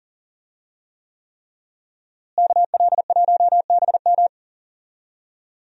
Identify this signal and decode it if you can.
Morse code, heard as a keyed tone, about 37 wpm, 700 Hz. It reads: KL1BM
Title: USA Call Signs - 5 Characters - Repeated 3x - 40wpm